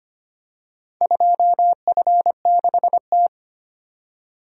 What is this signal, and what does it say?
Morse code, heard as a keyed tone, about 25 wpm, 700 Hz. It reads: 2F6T
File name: practice / 25wpm / Four Letter-Number Combinations - 25wpm